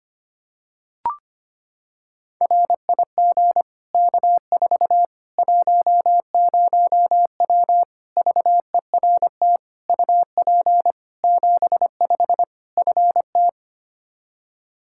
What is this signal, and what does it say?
Morse code, heard as a keyed tone, about 25 wpm, 700 Hz. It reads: RIG K4 10W VERT UP 75 FT
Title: Practice QSO - Exchange 2 - Info - 25wpm